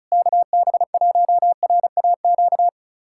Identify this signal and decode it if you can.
Morse code, heard as a keyed tone, about 36 wpm, 700 Hz. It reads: KB1RAQ
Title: USA Callsign DE Callsign 35wpm